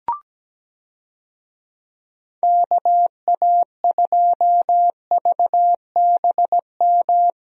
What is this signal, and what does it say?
Morse code, heard as a keyed tone, about 17 wpm, 700 Hz. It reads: KA2VBM
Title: USA Call Signs - 6 Characters - Repeated 2x - 17wpm